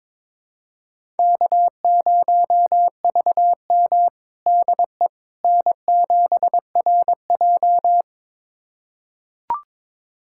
Morse code, heard as a keyed tone, about 22 wpm, 700 Hz. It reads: K0VM DE N7RJ
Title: USA Callsign DE Callsign - 4 Characters - 22wpm